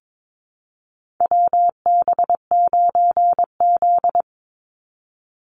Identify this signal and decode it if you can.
Morse code, heard as a keyed tone, about 22 wpm, 700 Hz. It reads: WB9Z
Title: Call Signs Repeated 2x - 22wpm